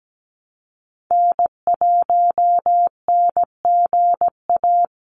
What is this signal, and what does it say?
Morse code, heard as a keyed tone, about 17 wpm, 700 Hz. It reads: N1NGA
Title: USA Call Signs 17wpm